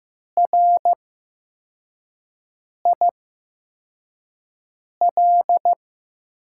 Morse code, heard as a keyed tone, about 15 wpm, spaced as for 4 wpm, 700 Hz. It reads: RIL
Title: USA Call Signs - 6 Characters - 8x Character Spacing - 15wpm